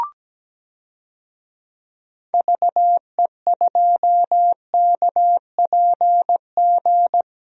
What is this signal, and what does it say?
Morse code, heard as a keyed tone, about 17 wpm, 700 Hz. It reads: VE2KPG
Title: DXCC-100 Call Signs - 6 Characters - 17wpm